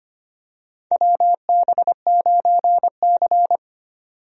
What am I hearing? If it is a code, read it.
Morse code, heard as a keyed tone, about 25 wpm, 700 Hz. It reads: WB9C
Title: USA Call Signs - 4 Characters - 25wpm